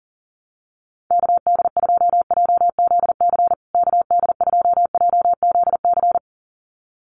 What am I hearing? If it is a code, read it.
Morse code, heard as a keyed tone, about 40 wpm, 700 Hz. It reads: KD2JZC KD2JZC
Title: USA Call Signs Repeated 2x 40wpm